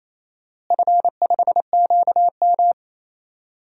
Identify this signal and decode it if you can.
Morse code, heard as a keyed tone, about 27 wpm, 700 Hz. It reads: F5QM
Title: DXCC-100 Call Signs - 4 Characters - Repeated 2x - 28wpm